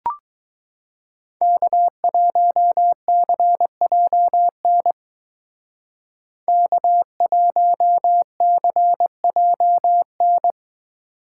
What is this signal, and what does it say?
Morse code, heard as a keyed tone, about 20 wpm, 700 Hz. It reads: K1CJN K1CJN